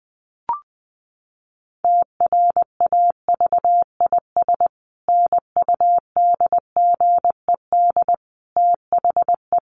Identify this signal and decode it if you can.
Morse code, heard as a keyed tone, about 20 wpm, 700 Hz. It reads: TRAVIS NUDGED THE
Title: A Sound Of Thunder 20wpm